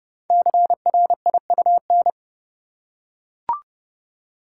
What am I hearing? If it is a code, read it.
Morse code, heard as a keyed tone, about 29 wpm, 690 Hz. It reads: CRIUN
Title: Code Groups - 5 Letters - 30wpm